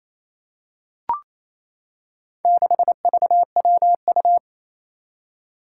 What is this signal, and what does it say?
Morse code, heard as a keyed tone, about 28 wpm, 700 Hz. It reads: 6VWU
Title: Four Challenging Letter-Number Combinations 28wpm